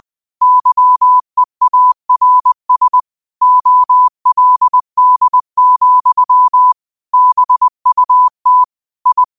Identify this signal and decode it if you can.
Morse code, heard as a keyed tone, about 20 wpm, 1000 Hz. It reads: YEARS OLD, BUT I
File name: qotdfiles / 20wpm